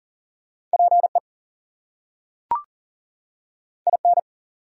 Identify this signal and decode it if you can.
Morse code, heard as a keyed tone, about 40 wpm, 700 Hz. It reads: PE IN
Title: US States and Canadian Provinces 40wpm